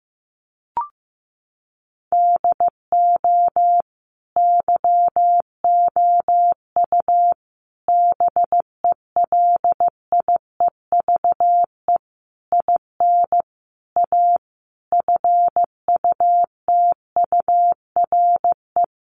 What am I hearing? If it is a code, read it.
Morse code, heard as a keyed tone, about 15 wpm, 700 Hz. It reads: DO YOU BELIEVE IN A FUTURE
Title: Sentences from Top 1000 Words 15wpm